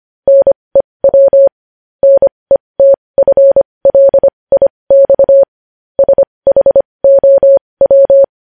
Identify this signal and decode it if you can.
Morse code, heard as a keyed tone, about 25 wpm, spaced as for 20 wpm, 550 Hz. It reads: NEW NETFLIX SHOW
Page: news-headlines-in-morse-code-at-20-wpm